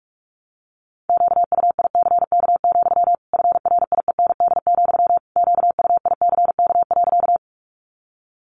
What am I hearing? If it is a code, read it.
Morse code, heard as a keyed tone, about 44 wpm, 700 Hz. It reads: QUICK, FRIEND, QUICK.
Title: Sentences from Top 500 Words 45wpm